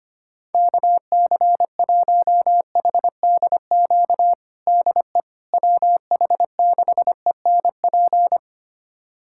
Morse code, heard as a keyed tone, about 25 wpm, 700 Hz. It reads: KC1HDQ DE WH6ENP
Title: USA Callsign DE Callsign - 6 Characters - 25wpm